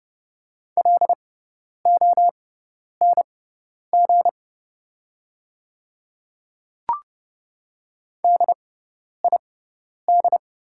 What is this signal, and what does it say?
Morse code, heard as a keyed tone, about 29 wpm, spaced as for 10 wpm, 690 Hz. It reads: LONG DID
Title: 6x Character Spacing - Top 100 Words - 30wpm